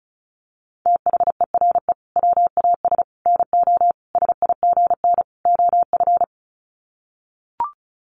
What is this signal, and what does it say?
Morse code, heard as a keyed tone, about 34 wpm, 700 Hz. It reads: THERE WAS NO SIGN OF
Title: Sets of 5 Words Encore2 35wpm